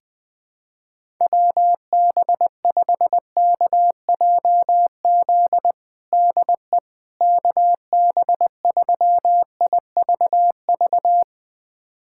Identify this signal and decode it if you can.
Morse code, heard as a keyed tone, about 20 wpm, 700 Hz. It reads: WB5KJZ DE KB3IVV
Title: USA Callsign DE Callsign - 6 Characters - 20wpm